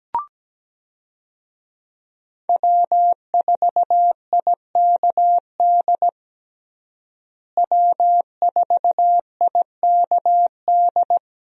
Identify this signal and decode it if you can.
Morse code, heard as a keyed tone, about 17 wpm, 700 Hz. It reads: W4IKD W4IKD
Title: USA Call Signs - 5 Characters - Repeated 2x - 17wpm